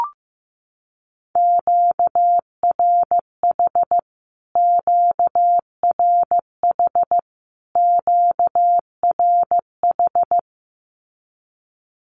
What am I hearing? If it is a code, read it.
Morse code, heard as a keyed tone, about 15 wpm, 700 Hz. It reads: QRH QRH QRH
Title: Q-Codes Repeated 3x 15wpm